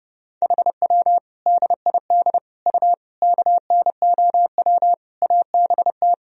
Morse code, heard as a keyed tone, about 32 wpm, 700 Hz. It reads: HW DID U KNOW ABT
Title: Sentences from Top 100 Words and Abbreviations 30wpm